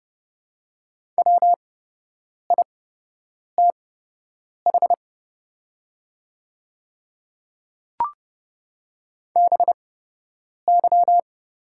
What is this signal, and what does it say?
Morse code, heard as a keyed tone, about 30 wpm, spaced as for 8 wpm, 700 Hz. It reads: WITH BY
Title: 8x Character Spacing - Top 200 Words - 30wpm